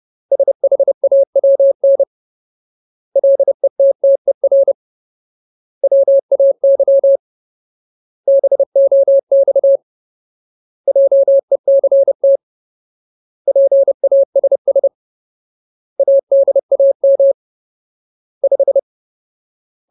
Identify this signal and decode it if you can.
Morse code, heard as a keyed tone, about 30 wpm, 550 Hz. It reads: SHAWN LETTER WAY BOX JECT PASS ADAM 5
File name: Ultimate Word List / 30WPM Ultimate